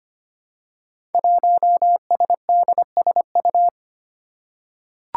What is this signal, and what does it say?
Morse code, heard as a keyed tone, about 25 wpm, 700 Hz. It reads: 1SDSU
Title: Five Challenging Letter-Number Combinations 25wpm